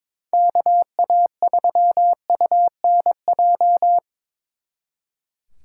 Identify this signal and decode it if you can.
Morse code, heard as a keyed tone, about 26 wpm, 700 Hz. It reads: KA3UNJ
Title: Speed Racing - USA Call Signs 22wpm